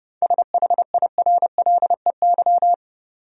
Morse code, heard as a keyed tone, about 32 wpm, 700 Hz. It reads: SHIRLEY
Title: USA Common Names - 30wpm